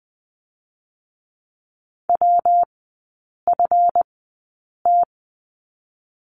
Morse code, heard as a keyed tone, about 20 wpm, spaced as for 8 wpm, 700 Hz. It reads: WFT
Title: 138 Lesson 23e - Letter f - Three Characters 20wpm